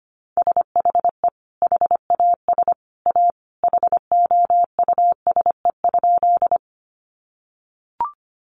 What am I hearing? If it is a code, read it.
Morse code, heard as a keyed tone, about 25 wpm, 700 Hz. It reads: SHE HAS A HOUSE?
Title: Sentences from Top 200 Words 25wpm